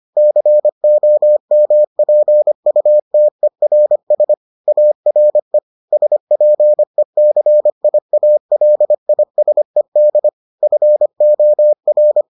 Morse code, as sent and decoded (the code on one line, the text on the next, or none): -.-. --- -- .--. ..- - . .-. ... / .- .-. / ... .--. -.-. .. .- .-.. .. ... -.. / ..-. --- .-.
COMPUTERS ARE SPECIALISED FOR